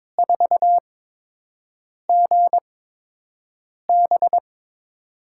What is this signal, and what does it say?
Morse code, heard as a keyed tone, about 22 wpm, spaced as for 6 wpm, 700 Hz. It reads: 4GB
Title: USA Call Signs - 8x Character Spacing - 22wpm